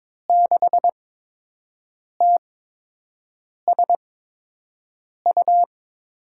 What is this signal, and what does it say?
Morse code, heard as a keyed tone, about 22 wpm, spaced as for 6 wpm, 700 Hz. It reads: 6TSU
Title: USA Call Signs - 8x Character Spacing - 22wpm